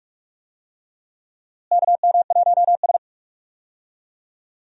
Morse code, heard as a keyed tone, about 46 wpm, 700 Hz. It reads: KM1S
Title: USA Call Signs - 4 Characters - Repeated 3x - 45wpm